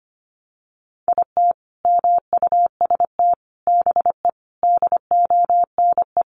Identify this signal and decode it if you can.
Morse code, heard as a keyed tone, about 25 wpm, 700 Hz. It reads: IT MUST BE DONE
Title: Sentences from Top 500 Words 25wpm